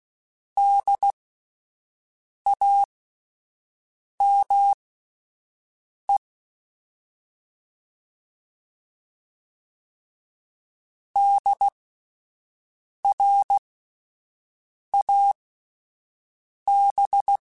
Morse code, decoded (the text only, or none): DAME DRAB